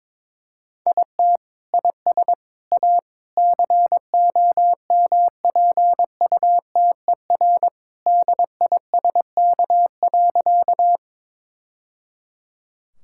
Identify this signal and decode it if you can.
Morse code, heard as a keyed tone, about 22 wpm, 700 Hz. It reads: IT IS A COMPUTER DISK.
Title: Sentences from Top 1000 Words - Review Part 6 - 22wpm